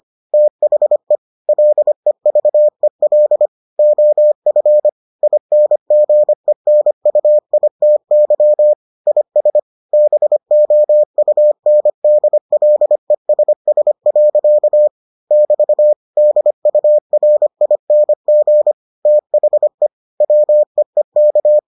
Morse code, heard as a keyed tone, about 25 wpm, 600 Hz. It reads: THE LEVEL OF INGENUITY IS BOUNDLESS. <BT> DURING THE WEEK